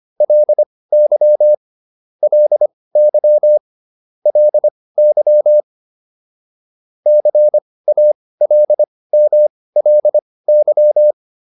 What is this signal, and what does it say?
Morse code, heard as a keyed tone, about 25 wpm, spaced as for 18 wpm, 600 Hz. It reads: LY LY LY CALMLY
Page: intermediate-practice-files